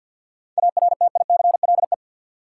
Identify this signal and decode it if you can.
Morse code, heard as a keyed tone, about 50 wpm, 700 Hz. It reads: ARTICLE
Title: Sentences from Top 1000 Words - Review Part 5 - New Word Drill - 50wpm